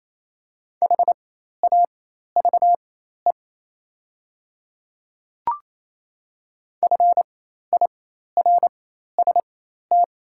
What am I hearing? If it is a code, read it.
Morse code, heard as a keyed tone, about 27 wpm, spaced as for 13 wpm, 690 Hz. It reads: HAVE FIRST
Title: 4x Character Spacing - Top 200 Words - 28wpm